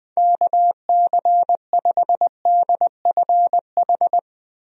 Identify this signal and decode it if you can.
Morse code, heard as a keyed tone, about 20 wpm, 700 Hz. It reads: KC5DFH
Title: USA Call Signs - 6 Characters - Repeated 3x - 20wpm